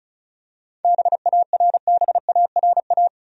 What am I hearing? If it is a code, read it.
Morse code, heard as a keyed tone, about 36 wpm, 700 Hz. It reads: BARBARA
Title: USA Common Names - 35wpm